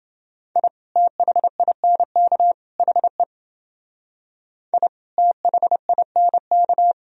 Morse code, heard as a keyed tone, about 28 wpm, 700 Hz. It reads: I THINK HE I THINK